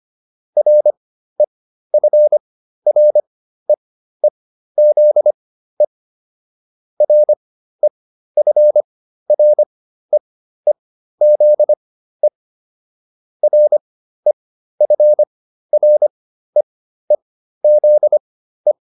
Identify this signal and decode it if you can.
Morse code, heard as a keyed tone, about 25 wpm, spaced as for 13 wpm, 600 Hz. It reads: REFREEZE REFREEZE REFREEZE
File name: RE8 13